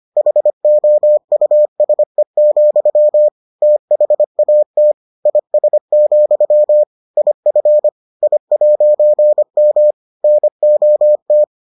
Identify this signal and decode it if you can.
Morse code, heard as a keyed tone, about 25 wpm, 600 Hz. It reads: HOUSE, THAT IS, IF I'M NOT